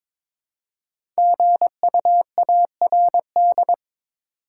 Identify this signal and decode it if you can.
Morse code, heard as a keyed tone, about 22 wpm, 700 Hz. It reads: GUARD
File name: 5 Letter Words - First 3 Letters Given - 22wpm